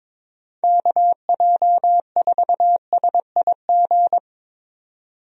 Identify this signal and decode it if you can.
Morse code, heard as a keyed tone, about 22 wpm, 700 Hz. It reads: KJ4SIG
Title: USA Call Signs - 6 Characters - 22wpm